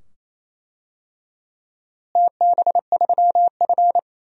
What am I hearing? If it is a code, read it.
Morse code, heard as a keyed tone, about 28 wpm, 700 Hz. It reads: TB3F
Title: DXCC-100 Call Signs - 4 Characters - Repeated 2x - 28wpm